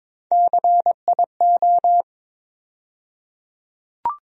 Morse code, heard as a keyed tone, about 21 wpm, 700 Hz. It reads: CIO
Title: USA Call Sign Postfixes - 3 Letters 22wpm